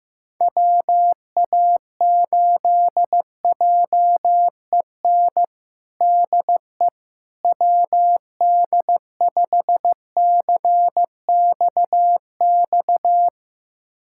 Morse code, heard as a keyed tone, about 15 wpm, 700 Hz. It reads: WA8JEN DE WD5CXX
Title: USA Callsign DE Callsign 15wpm